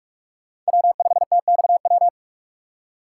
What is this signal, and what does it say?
Morse code, heard as a keyed tone, about 47 wpm, 700 Hz. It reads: W5TXW
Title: USA Call Signs - 3x Character Spacing - 45wpm